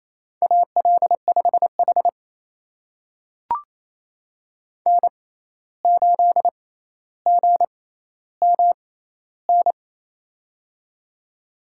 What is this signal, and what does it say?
Morse code, heard as a keyed tone, about 28 wpm, spaced as for 13 wpm, 700 Hz. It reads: AL5H N8GMN
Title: USA Call Signs - 6x Character Spacing - 28wpm